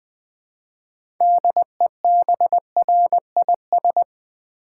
Morse code, heard as a keyed tone, about 20 wpm, 700 Hz. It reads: DEBRIS
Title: Build A Word - 3 Letters At A Time - 6-Letter Words - 20wpm